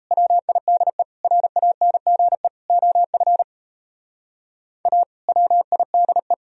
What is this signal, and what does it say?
Morse code, heard as a keyed tone, about 34 wpm, 700 Hz. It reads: WIDE RANGE OF A WIDE
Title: Speed-Racing - Sets of 4 Words 28wpm